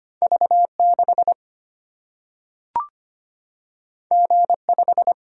Morse code, heard as a keyed tone, about 25 wpm, 700 Hz. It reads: V6 G5
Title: Single-Letter Single-Number 25wpm